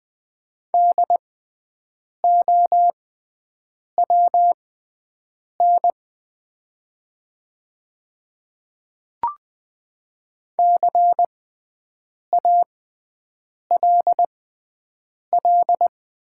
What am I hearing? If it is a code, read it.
Morse code, heard as a keyed tone, about 20 wpm, spaced as for 7 wpm, 700 Hz. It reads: DOWN CALL
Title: 6x Character Spacing - Top 300 Words - 20wpm